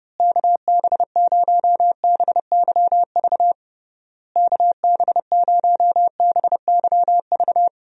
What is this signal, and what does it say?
Morse code, heard as a keyed tone, about 30 wpm, 700 Hz. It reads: KB0BYV KB0BYV
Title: USA Call Signs - 6 Characters - Repeated 3x - 30wpm